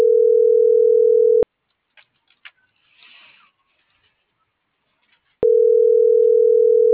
Ringback tone (North America), 440 + 480 Hz.